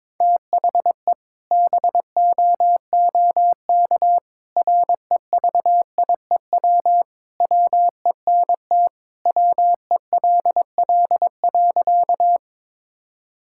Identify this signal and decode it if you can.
Morse code, heard as a keyed tone, about 21 wpm, 700 Hz. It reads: THE BOOK REVIEW WENT WELL.
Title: Sentences from Top 1000 Words - Review Part 6 - 22wpm